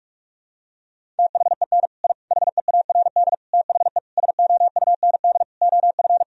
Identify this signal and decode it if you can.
Morse code, heard as a keyed tone, about 46 wpm, 700 Hz. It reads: THEN I HEARD THE SOUND OF
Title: Sentences from Top 1000 Words - Review Part 1 - 45wpm